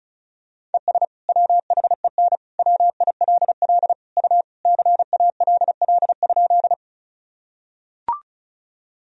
Morse code, heard as a keyed tone, about 34 wpm, 700 Hz. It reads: ES WHEN WILL U CALL?
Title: Sentences from Top 100 Words and Abbreviations 35wpm